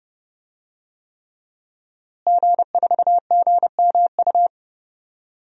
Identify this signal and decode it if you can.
Morse code, heard as a keyed tone, about 30 wpm, 700 Hz. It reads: G4GMU
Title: DXCC-100 Call Signs - 5 Characters - Repeated 3x - 30wpm